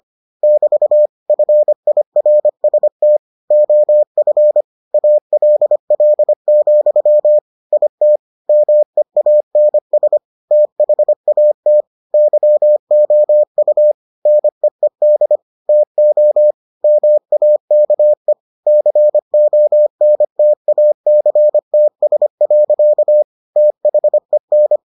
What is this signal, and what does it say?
Morse code, heard as a keyed tone, about 25 wpm, 600 Hz. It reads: <BT> FIRST OF ALL, IT MEANS THAT YOU NEED TO MAKE CONTACTS. THEN